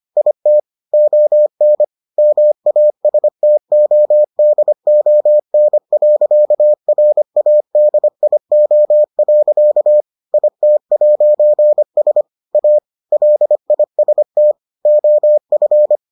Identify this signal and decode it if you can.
Morse code, heard as a keyed tone, about 25 wpm, 600 Hz. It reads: IT ON MASTODON.RADIO. IT'S A LIST OF